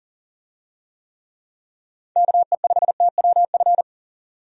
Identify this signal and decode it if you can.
Morse code, heard as a keyed tone, about 40 wpm, 700 Hz. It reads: KE5TWF
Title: USA Call Signs - 6 Characters - 40wpm